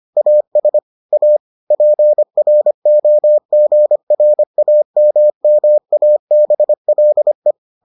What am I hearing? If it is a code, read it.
Morse code, heard as a keyed tone, about 25 wpm, 600 Hz. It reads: AS A PROGRAMMABLE